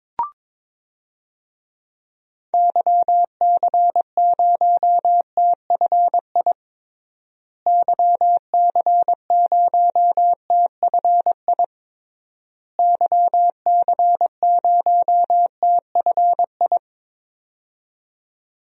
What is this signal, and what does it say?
Morse code, heard as a keyed tone, about 22 wpm, 700 Hz. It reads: YC0TFI YC0TFI YC0TFI
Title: DXCC-100 Call Signs - 6 Characters - Repeated 3x - 22wpm